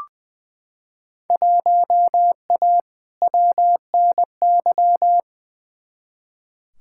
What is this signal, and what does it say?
Morse code, heard as a keyed tone, about 21 wpm, 700 Hz. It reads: 1A WNY
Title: Field Day Exchanges Encore - 20wpm